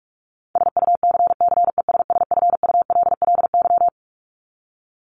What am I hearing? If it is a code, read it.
Morse code, heard as a keyed tone, about 44 wpm, 700 Hz. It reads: SUCCESSFULLY
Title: LY Suffix - Top 100 Words with Base Words 45wpm